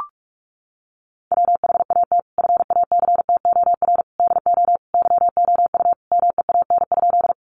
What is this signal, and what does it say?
Morse code, heard as a keyed tone, about 44 wpm, 700 Hz. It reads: WHAT FACTOR DO YOU MEAN?
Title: Sentences from Top 1300 Words 45wpm